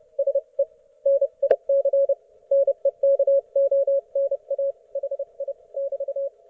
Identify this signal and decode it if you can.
Morse code, heard as a keyed tone, about 30 wpm, 550 Hz. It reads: SE NIC NEKONA HI <BT>
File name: Archiv / CUC181 / ok1fhi cul181